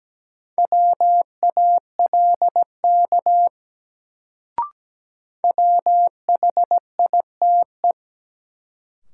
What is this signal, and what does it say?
Morse code, heard as a keyed tone, about 17 wpm, 700 Hz. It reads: WALK WHITE
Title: Sentences from Top 300 Words - New Word Drill - 17wpm